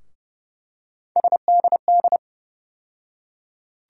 Morse code, heard as a keyed tone, about 30 wpm, 700 Hz. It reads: SDD
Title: USA Call Sign Postfixes - 3 Letters 30wpm